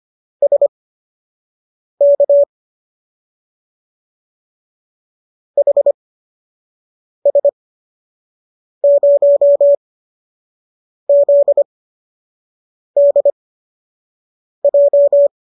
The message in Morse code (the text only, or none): SK HS0ZDJ